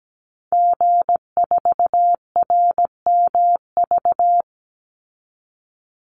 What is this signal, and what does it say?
Morse code, heard as a keyed tone, about 17 wpm, 700 Hz. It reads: G4RMV
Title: Call Signs Repeated 3x - 17wpm